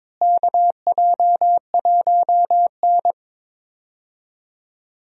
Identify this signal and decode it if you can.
Morse code, heard as a keyed tone, about 22 wpm, 700 Hz. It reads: KJ1N
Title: USA Call Signs - 4 Characters - Repeated 3x - 22wpm